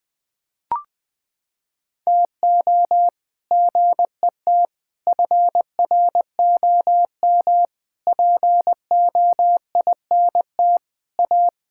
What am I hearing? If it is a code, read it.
Morse code, heard as a keyed tone, about 20 wpm, 700 Hz. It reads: TO GET FROM POINT A
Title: Sentences with 8-Word Phrases - 20wpm